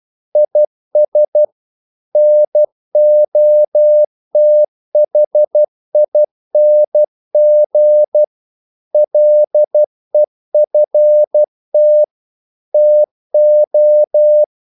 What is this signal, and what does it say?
Morse code, heard as a keyed tone, about 12 wpm, 600 Hz. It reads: IS NOTHING LEFT TO
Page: English quotations in CW (telegraphy - morse)